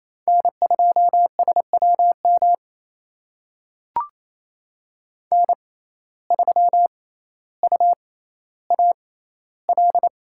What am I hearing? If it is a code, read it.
Morse code, heard as a keyed tone, about 28 wpm, spaced as for 14 wpm, 700 Hz. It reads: N2SWM N3UAL
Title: USA Call Signs - 5 Characters - 6x Character Spacing - 28wpm